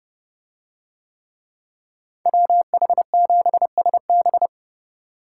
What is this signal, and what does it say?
Morse code, heard as a keyed tone, about 29 wpm, 700 Hz. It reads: WH7SB